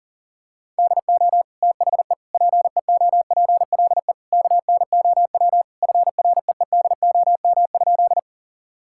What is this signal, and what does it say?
Morse code, heard as a keyed tone, about 40 wpm, 700 Hz. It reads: DO THE PEOPLE KNOW FREEDOM?
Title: Sentences from Top 1000 Words - Review Part 5 - 40wpm